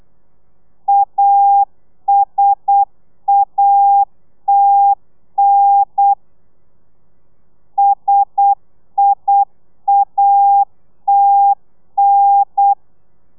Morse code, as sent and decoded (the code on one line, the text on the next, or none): .- ... .- - -. / ... .. .- - -.
ASATN SIATN